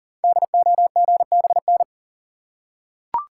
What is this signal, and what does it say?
Morse code, heard as a keyed tone, about 39 wpm, 690 Hz. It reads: DOGBN
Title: Code Groups - 5 Letters - 40wpm